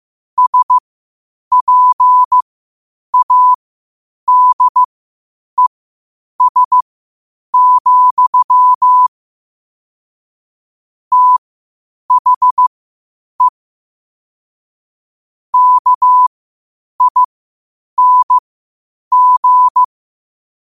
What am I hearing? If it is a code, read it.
Morse code, heard as a keyed tone, about 15 wpm, spaced as for 8 wpm, 1000 Hz. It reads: SPADES, THE KING
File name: qotdfiles / 8wpm